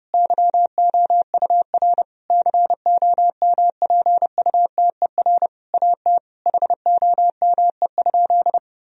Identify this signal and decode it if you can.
Morse code, heard as a keyed tone, about 30 wpm, 700 Hz. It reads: YOUR COMPUTER AT HOME?
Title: Sentences from Top 1000 Words - Review Part 3 - 30wpm